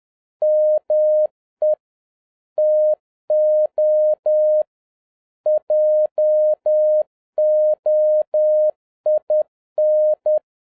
Morse code, heard as a keyed tone, about 10 wpm, 610 Hz. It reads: ME TO JOIN